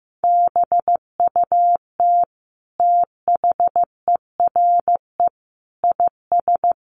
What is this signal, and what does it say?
Morse code, heard as a keyed tone, about 14 wpm, 700 Hz. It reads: BUT THERE IS
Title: Sentences from Top 1300 Words 15wpm